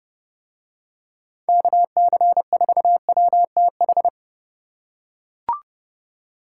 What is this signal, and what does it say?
Morse code, heard as a keyed tone, about 28 wpm, 700 Hz. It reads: KC4WTH